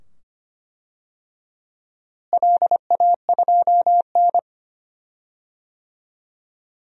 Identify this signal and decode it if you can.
Morse code, heard as a keyed tone, about 25 wpm, 700 Hz. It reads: LA2N
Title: DXCC-100 Call Signs - 4 Characters - Repeated 2x - 25wpm